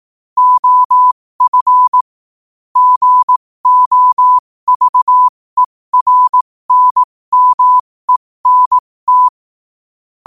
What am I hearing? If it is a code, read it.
Morse code, heard as a keyed tone, about 18 wpm, spaced as for 15 wpm, 1000 Hz. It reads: OF GOVERNMENT
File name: qotdfiles / 15wpm